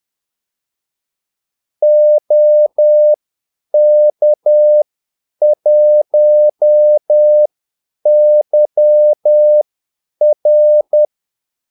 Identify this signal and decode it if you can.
Morse code, heard as a keyed tone, about 10 wpm, spaced as for 8 wpm, 600 Hz. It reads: OK1YR
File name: Callsigns-10-8wpm